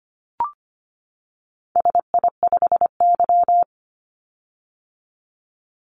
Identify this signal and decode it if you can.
Morse code, heard as a keyed tone, about 25 wpm, 700 Hz. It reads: SI5Y
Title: Call Signs - 25wpm